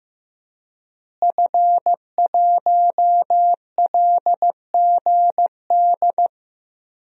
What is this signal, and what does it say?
Morse code, heard as a keyed tone, about 15 wpm, 700 Hz. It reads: F1LGD
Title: DXCC-100 Call Signs - 5 Characters - Repeated 3x - 15wpm